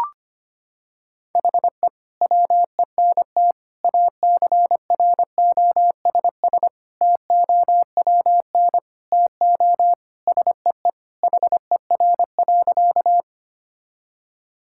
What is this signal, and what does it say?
Morse code, heard as a keyed tone, about 25 wpm, 700 Hz. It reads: HE WENT ACROSS TOWN TO SEE HER.
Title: Sentences from Top 1000 Words - Review Part 5 - 25wpm